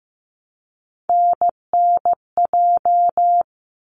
Morse code, heard as a keyed tone, about 15 wpm, 700 Hz. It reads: NNJ
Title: ARRL Sections 15wpm